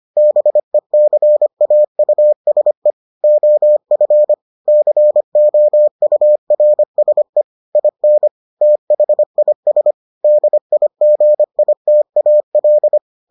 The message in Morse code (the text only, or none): BECAUSE OF COURSE IN THIS DIGITAL